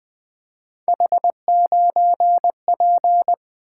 Morse code, heard as a keyed tone, about 21 wpm, 700 Hz. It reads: H9P
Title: Three Letter-Number Combinations - 20wpm